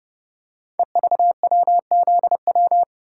Morse code, heard as a keyed tone, about 29 wpm, 700 Hz. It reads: EVWZW
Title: Code Groups - 5 Letter:Numbers - 30wpm